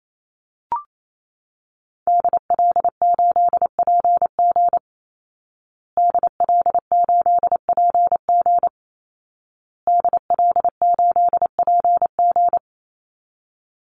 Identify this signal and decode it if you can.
Morse code, heard as a keyed tone, about 28 wpm, 700 Hz. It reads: DL8PG DL8PG DL8PG
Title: Call Signs Repeated 2x - 28wpm